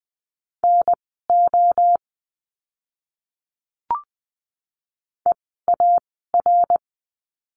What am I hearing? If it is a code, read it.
Morse code, heard as a keyed tone, about 20 wpm, spaced as for 15 wpm, 700 Hz. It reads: NO EAR
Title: 054 Lesson 10f - Letter r - Words 20wpm